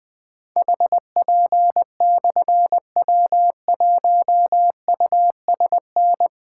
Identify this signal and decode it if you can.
Morse code, heard as a keyed tone, about 20 wpm, 700 Hz. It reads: HP/W1USN
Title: Difficult Call Signs 20wpm